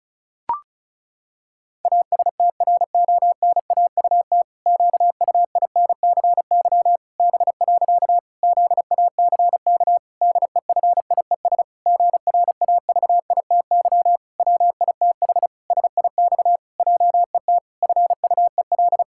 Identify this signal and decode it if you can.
Morse code, heard as a keyed tone, about 36 wpm, 700 Hz. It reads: ASTRONAUT QUINCY B. ZACK DEFIES GRAVITY WITH SIX JET FUEL
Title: Pangram Challenge v2 - 35wpm